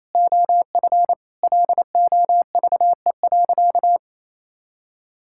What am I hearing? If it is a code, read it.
Morse code, heard as a keyed tone, about 28 wpm, 700 Hz. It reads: OF LOVE.
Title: Sentences from Top 1000 Words - Review Part 5 - 28wpm